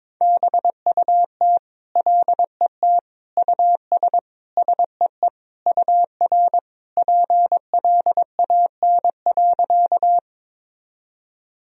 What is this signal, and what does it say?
Morse code, heard as a keyed tone, about 22 wpm, 700 Hz. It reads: BUT LET US SEE UR PLAN.
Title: Sentences from Top 500 Words and Abbreviations 22wpm